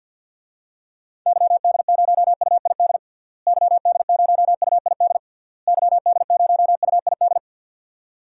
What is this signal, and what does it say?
Morse code, heard as a keyed tone, about 46 wpm, 700 Hz. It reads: YD0UID YD0UID YD0UID
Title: DXCC-100 Call Signs - 6 Characters - Repeated 3x - 50wpm